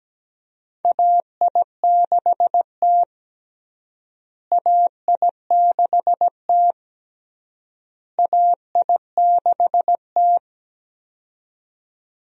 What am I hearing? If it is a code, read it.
Morse code, heard as a keyed tone, about 17 wpm, 700 Hz. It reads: AI6T AI6T AI6T
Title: USA Call Signs - 4 Characters - Repeated 3x - 17wpm